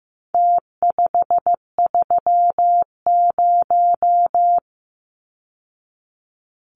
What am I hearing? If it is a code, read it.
Morse code, heard as a keyed tone, about 15 wpm, 700 Hz. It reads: T530
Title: Numbers - Contest Style - 4-Digit 15wpm